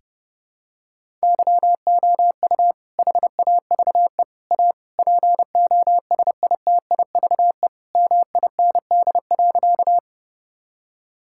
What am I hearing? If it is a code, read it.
Morse code, heard as a keyed tone, about 30 wpm, 700 Hz. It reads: YOU HAVE A POSITIVE MIND.
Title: Sentences from Top 1000 Words - Review Part 5 - 30wpm